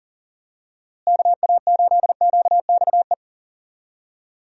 Morse code, heard as a keyed tone, about 37 wpm, 700 Hz. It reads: KA8QXE
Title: USA Call Signs - 6 Characters - Repeated 3x - 40wpm